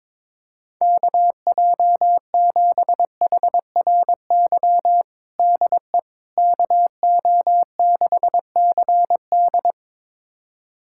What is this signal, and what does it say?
Morse code, heard as a keyed tone, about 22 wpm, 700 Hz. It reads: KJ7HRY DE KO6CD
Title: USA Callsign DE Callsign 22wpm